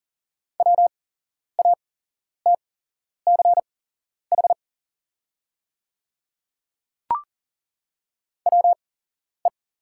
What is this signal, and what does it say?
Morse code, heard as a keyed tone, about 40 wpm, spaced as for 11 wpm, 700 Hz. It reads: WATCH WE
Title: 8x Character Spacing - Top 300 Words - 40wpm